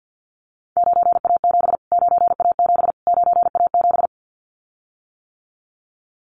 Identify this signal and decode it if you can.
Morse code, heard as a keyed tone, about 50 wpm, 700 Hz. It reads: 9A7 9A7 9A7
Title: Callsign Prefixes for Top 25 DXCC - Repeated 3x - 50wpm